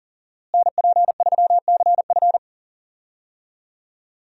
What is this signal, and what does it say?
Morse code, heard as a keyed tone, about 40 wpm, 700 Hz. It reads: NP3CF
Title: USA Call Signs - 5 Characters - Repeated 2x - 40wpm